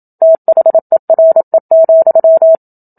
Morse code, heard as a keyed tone, about 27 wpm, 650 Hz. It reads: THERE,
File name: Book Two / 01-Chapter 1 - 27 WPM